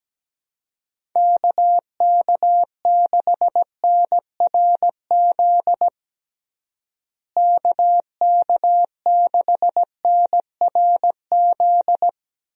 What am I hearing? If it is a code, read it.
Morse code, heard as a keyed tone, about 17 wpm, 700 Hz. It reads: KK6NRZ KK6NRZ
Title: USA Call Signs - 6 Characters - Repeated 2x - 17wpm